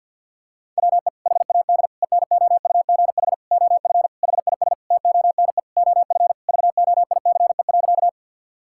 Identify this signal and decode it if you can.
Morse code, heard as a keyed tone, about 50 wpm, 700 Hz. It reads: WE HAD ENOUGH OF HIS TONE OF VOICE.
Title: Sentences from Top 1600 Words 50wpm